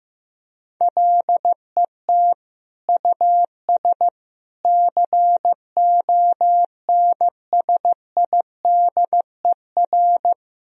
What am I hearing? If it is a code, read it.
Morse code, heard as a keyed tone, about 16 wpm, 700 Hz. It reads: LET US CONSIDER
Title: Sentences from Top 1000 Words - Review Part 4 - 15wpm